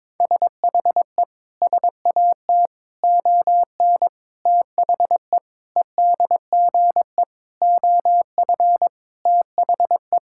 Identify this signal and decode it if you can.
Morse code, heard as a keyed tone, about 21 wpm, 700 Hz. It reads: SHE SAT ON THE EDGE OF THE
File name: practice / 22wpm / Sentences with 4-Word Phrases - 22wpm